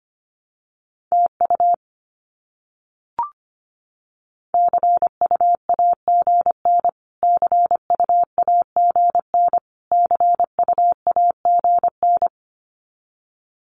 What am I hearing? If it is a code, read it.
Morse code, heard as a keyed tone, about 25 wpm, 700 Hz. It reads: TU CUAGN CUAGN CUAGN
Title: Abbreviations Repeated 3x - 25wpm